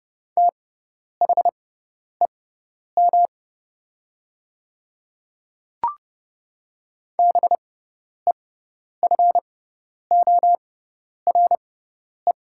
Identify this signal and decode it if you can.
Morse code, heard as a keyed tone, about 30 wpm, spaced as for 10 wpm, 700 Hz. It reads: THEM BEFORE
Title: 6x Character Spacing - Top 200 Words - 30wpm